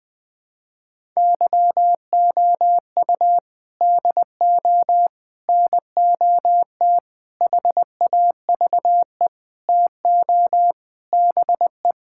Morse code, heard as a keyed tone, about 20 wpm, 700 Hz. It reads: YOU DO NOT HAVE TO BE
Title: Sentences with 5-Word Phrases - 20wpm